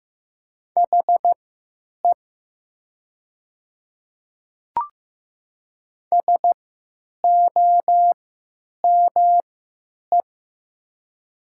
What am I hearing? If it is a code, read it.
Morse code, heard as a keyed tone, about 15 wpm, spaced as for 9 wpm, 700 Hz. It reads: HE SOME
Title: 3x Character Spacing - Top 100 Words - 15wpm